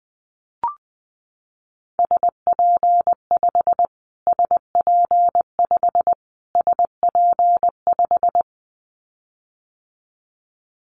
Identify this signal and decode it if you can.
Morse code, heard as a keyed tone, about 20 wpm, 700 Hz. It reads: SP5 SP5 SP5
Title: Callsign Prefixes for Top 25 DXCC - Repeated 3x - 20wpm